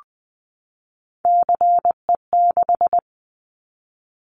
Morse code, heard as a keyed tone, about 20 wpm, 700 Hz. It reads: CE6
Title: Callsign Prefixes for Top 100 DXCC - 20wpm